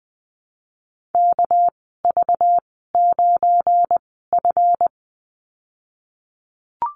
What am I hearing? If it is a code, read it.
Morse code, heard as a keyed tone, about 20 wpm, spaced as for 15 wpm, 700 Hz. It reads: KV9F
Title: 238 Lesson 38g - Letter q - Call Signs 20wpm